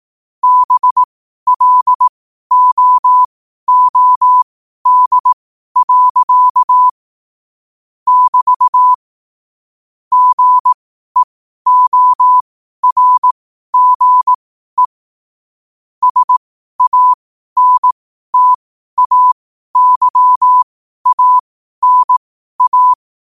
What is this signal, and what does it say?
Morse code, heard as a keyed tone, about 18 wpm, spaced as for 12 wpm, 1000 Hz. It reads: BLOOD. <BT> GEORGE SANTAYANA
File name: qotdfiles / 12wpm